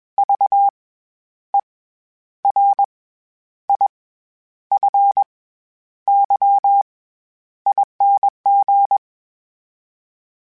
Morse code, heard as a keyed tone, about 21 wpm, 800 Hz. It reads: V E R I F Y ING
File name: Special INGs-mp3s / Special INGs-mp3s / SlowToFaster ing @20